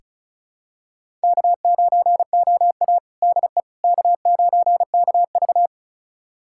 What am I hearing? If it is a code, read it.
Morse code, heard as a keyed tone, about 36 wpm, 700 Hz. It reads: K9OA DE K9KV
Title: USA Callsign DE Callsign - 4 Characters - 35wpm